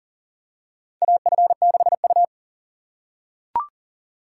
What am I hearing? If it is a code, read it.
Morse code, heard as a keyed tone, about 40 wpm, 700 Hz. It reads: AF6U
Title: USA Call Signs - 4 Characters - 4x Character Spacing - 40wpm